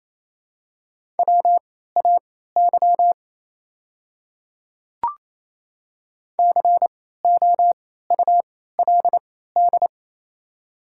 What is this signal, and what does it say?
Morse code, heard as a keyed tone, about 28 wpm, spaced as for 16 wpm, 700 Hz. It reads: WAY COULD
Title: 3x Character Spacing - Top 100 Words - 28wpm